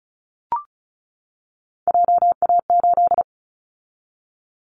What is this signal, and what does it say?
Morse code, heard as a keyed tone, about 36 wpm, 690 Hz. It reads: JA8
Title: Callsign Prefixes for Top 50 DXCC - 35wpm